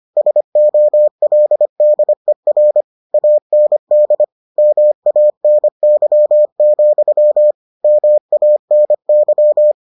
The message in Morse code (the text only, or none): SOLDER AND MANY, MANY